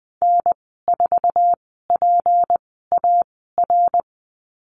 Morse code, heard as a keyed tone, about 20 wpm, spaced as for 14 wpm, 700 Hz. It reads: N4PAR
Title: 193 Lesson 31g - Letter v - Call Signs 20wpm